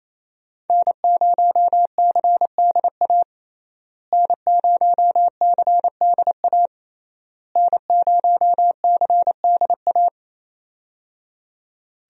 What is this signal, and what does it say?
Morse code, heard as a keyed tone, about 27 wpm, 700 Hz. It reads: N0CDA N0CDA N0CDA
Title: USA Call Signs - 5 Characters - Repeated 3x - 28wpm